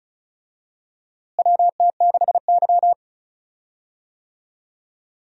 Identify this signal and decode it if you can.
Morse code, heard as a keyed tone, about 35 wpm, 700 Hz. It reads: WT6Y
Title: USA Call Signs - 4 Characters - Repeated 2x - 35wpm